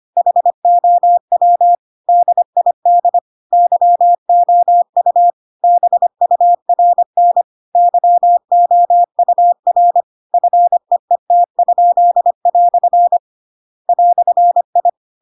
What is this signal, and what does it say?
Morse code, heard as a keyed tone, about 25 wpm, 700 Hz. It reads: HOW DID YOU BURN YOUR FEET?" "I
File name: Pinocchio / Pinocchio ch7 25 25 ik1hge